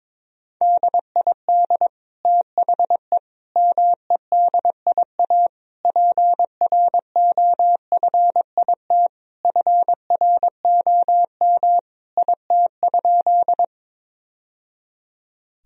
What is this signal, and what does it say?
Morse code, heard as a keyed tone, about 21 wpm, 700 Hz. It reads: DID THE MEDIA PROFIT FROM IT?
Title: Sentences from Top 1000 Words - Review Part 4 - 22wpm